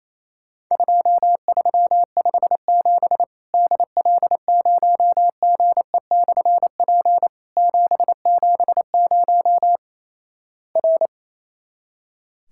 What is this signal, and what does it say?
Morse code, heard as a keyed tone, about 28 wpm, 700 Hz. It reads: 2357 DL0GE/P 770 R
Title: Worked All Europe DX Contest - QTC Exchange - 28wpm